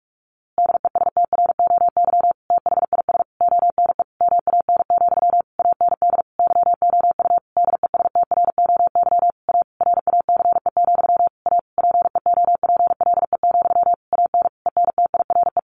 Morse code, heard as a keyed tone, about 44 wpm, 700 Hz. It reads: DESTROY THIS ONE MAN, AND YOU DESTROY A RACE, A PEOPLE, AN ENTIRE
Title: A Sound Of Thunder 45wpm